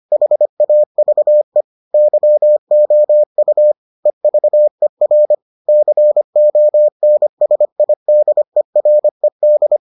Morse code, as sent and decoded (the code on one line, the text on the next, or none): .... .- ...- . / -.-- --- ..- / . ...- . .-. / -.-. --- -. ... .. -.. .-. -..
HAVE YOU EVER CONSIDERED